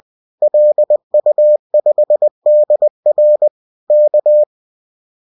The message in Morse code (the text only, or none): LU5DR K